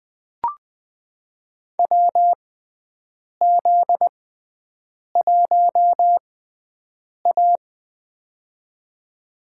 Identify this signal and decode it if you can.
Morse code, heard as a keyed tone, about 20 wpm, spaced as for 7 wpm, 700 Hz. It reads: WZ1A
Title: USA Call Signs - 4 Characters - 6x Character Spacing - 20wpm